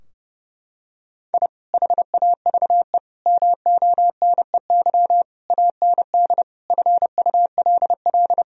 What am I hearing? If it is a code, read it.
Morse code, heard as a keyed tone, about 30 wpm, 700 Hz. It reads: I HAVE MONEY AND FULL
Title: Sentences from Top 1000 Words - Review Part 3 - 30wpm